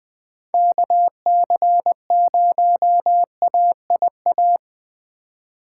Morse code, heard as a keyed tone, about 20 wpm, 700 Hz. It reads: KC0AIA
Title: USA Call Signs - 6 Characters - 3x Character Spacing - 20wpm